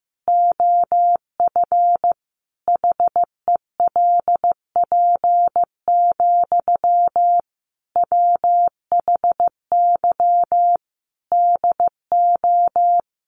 Morse code, heard as a keyed tone, about 15 wpm, 700 Hz. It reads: OF HELP, WHY DO
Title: Sentences from Top 1200 Words 15wpm